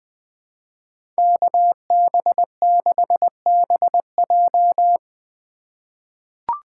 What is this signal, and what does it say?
Morse code, heard as a keyed tone, about 20 wpm, 700 Hz. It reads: KB6BJ
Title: USA Call Signs - 4x Character Spacing - 20wpm